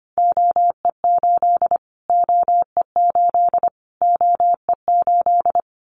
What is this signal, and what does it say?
Morse code, heard as a keyed tone, about 25 wpm, 700 Hz. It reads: OE8 OE8 OE8
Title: Callsign Prefixes for Top 25 DXCC - Repeated 3x - 25wpm